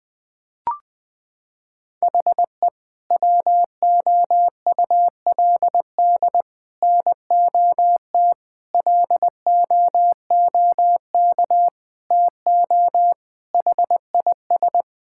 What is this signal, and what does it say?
Morse code, heard as a keyed tone, about 20 wpm, 700 Hz. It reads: HE WOULD NOT LOOK TO HIS
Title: Sentences from Top 1000 Words - Review Part 6 - 20wpm